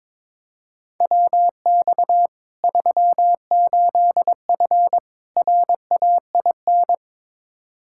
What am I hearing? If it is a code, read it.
Morse code, heard as a keyed tone, about 22 wpm, 700 Hz. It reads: WX 38F RAIN
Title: Practice QSO - Exchange 2 - Info - 22wpm